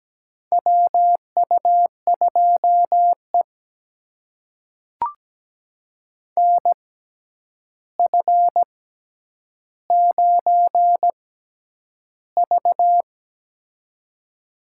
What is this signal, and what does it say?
Morse code, heard as a keyed tone, about 17 wpm, spaced as for 9 wpm, 700 Hz. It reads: WU2E NF9V
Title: USA Call Signs - 4 Characters - 6x Character Spacing - 17wpm